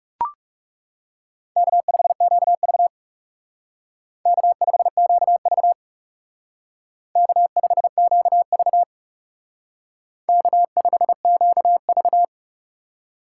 Morse code, heard as a keyed tone, about 37 wpm, 700 Hz. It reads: K5QV K5QV K5QV K5QV
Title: Speed Racing - USA 1x2 Call Signs 30wpm